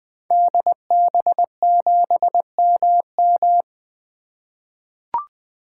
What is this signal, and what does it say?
Morse code, heard as a keyed tone, about 20 wpm, 700 Hz. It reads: DB7MM
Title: 2023 SOTA Activator Call Signs - 20wpm